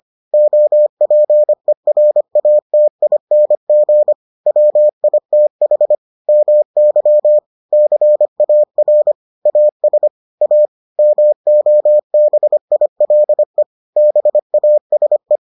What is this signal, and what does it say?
Morse code, heard as a keyed tone, about 25 wpm, 600 Hz. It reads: OPERATING WITH MY CAR AS A MOBILE BASE